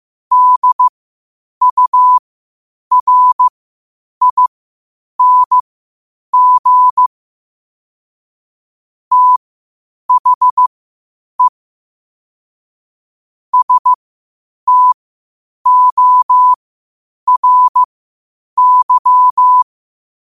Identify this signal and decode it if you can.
Morse code, heard as a keyed tone, about 15 wpm, spaced as for 8 wpm, 1000 Hz. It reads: DURING THE STORY
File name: qotdfiles / 8wpm